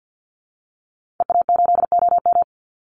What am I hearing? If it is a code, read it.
Morse code, heard as a keyed tone, about 48 wpm, 700 Hz. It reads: EA8OM
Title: Call Signs Repeated 3x - 50wpm